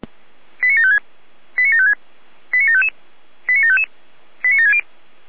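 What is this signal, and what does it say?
Five-tone selective call (ZVEI), heard as quick runs of five tones, tones of about 80 ms. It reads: 89756 89756 89755 89755 89760
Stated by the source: ZVEI five-tone selective call received by SDR.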